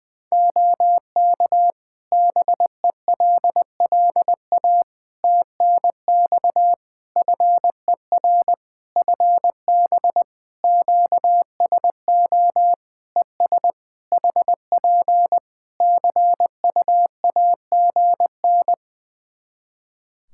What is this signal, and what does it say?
Morse code, heard as a keyed tone, about 20 wpm, 700 Hz. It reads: OK BELLA TNX FER FB QSO ES HP CUAGN
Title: K1USN SST QSO Practice 20wpm